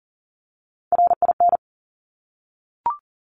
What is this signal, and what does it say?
Morse code, heard as a keyed tone, about 39 wpm, 700 Hz. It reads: RIN
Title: Top Three Letter Combinations in Words 40wpm